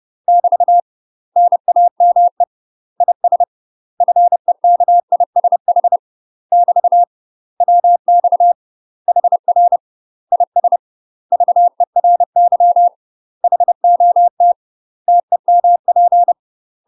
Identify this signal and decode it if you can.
Morse code, heard as a keyed tone, about 30 wpm, 700 Hz. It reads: <BT> NAME IS FEKISH <BT> WX HR IS VERY HOT TEMP
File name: UK MASTER QSOs TRAINING / 30WPM / UK MASTER QSOs 0033